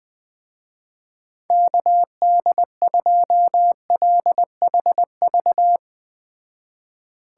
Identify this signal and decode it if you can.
Morse code, heard as a keyed tone, about 20 wpm, 700 Hz. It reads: KD2LHV